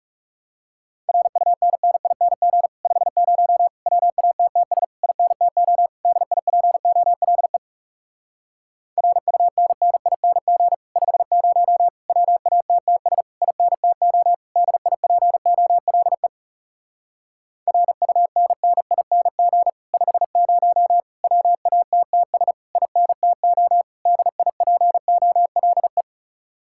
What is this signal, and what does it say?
Morse code, heard as a keyed tone, about 40 wpm, 700 Hz. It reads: RUNNING 50 WATTS INTO DIPOLE RUNNING 50 WATTS INTO DIPOLE RUNNING 50 WATTS INTO DIPOLE